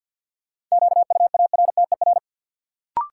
Morse code, heard as a keyed tone, about 51 wpm, 700 Hz. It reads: QUARTER
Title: Sentences from Top 1300 Words - New Word Drill Alt - 50wpm